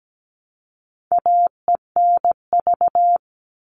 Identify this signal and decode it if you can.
Morse code, heard as a keyed tone, about 17 wpm, 700 Hz. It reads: AENV
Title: All Cut Numbers - 4-Digit 17wpm